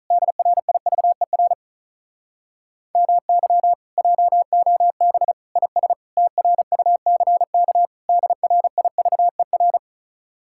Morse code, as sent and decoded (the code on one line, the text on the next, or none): -.. .-. .. ...- . .-. / -- -.-- / .--- --- -... / .. ... / - .-. ..- -.-. -.- / -.. .-. .. ...- . .-.
DRIVER MY JOB IS TRUCK DRIVER